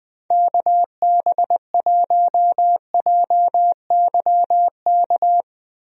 Morse code, heard as a keyed tone, about 20 wpm, 700 Hz. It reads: KB1JYK
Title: USA Call Signs - 6 Characters - Repeated 2x - 20wpm